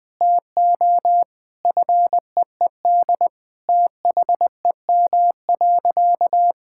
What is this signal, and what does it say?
Morse code, heard as a keyed tone, about 20 wpm, 700 Hz. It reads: TO FEED THEM.
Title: Sentences from Top 1600 Words 20wpm